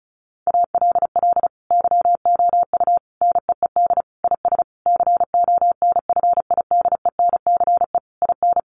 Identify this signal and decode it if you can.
Morse code, heard as a keyed tone, about 34 wpm, 700 Hz. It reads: ALL YOU NEED IS CONFIDENCE IN
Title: Sentences from Top 1500 Words 35wpm